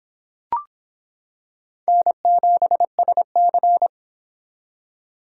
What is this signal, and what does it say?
Morse code, heard as a keyed tone, about 26 wpm, 700 Hz. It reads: N7SC